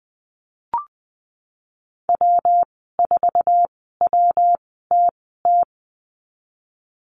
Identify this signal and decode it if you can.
Morse code, heard as a keyed tone, about 20 wpm, spaced as for 15 wpm, 700 Hz. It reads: W4WTT
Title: 122 Lesson 20g - Number 3 - Call Signs 20wpm